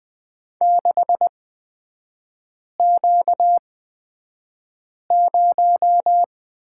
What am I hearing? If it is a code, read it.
Morse code, heard as a keyed tone, about 20 wpm, spaced as for 5 wpm, 700 Hz. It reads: 6Q0
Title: 236 Lesson 38e - Letter q - Three Characters 20wpm